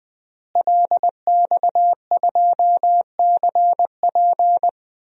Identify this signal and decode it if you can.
Morse code, heard as a keyed tone, about 20 wpm, 700 Hz. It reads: LX2CP
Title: DXCC-100 Call Signs - 5 Characters - Repeated 3x - 20wpm